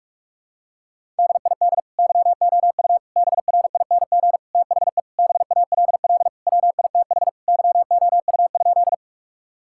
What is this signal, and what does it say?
Morse code, heard as a keyed tone, about 46 wpm, 700 Hz. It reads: DID YOU BRING THE BALL WITH YOU?
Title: Sentences from Top 1000 Words - Review Part 6 - 45wpm